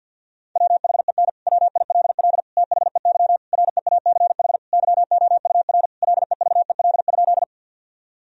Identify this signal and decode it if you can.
Morse code, heard as a keyed tone, about 50 wpm, 700 Hz. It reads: WHEN WILL THEY REACH YOUR LEVEL?
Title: Sentences from Top 1000 Words - Review Part 4 - 50wpm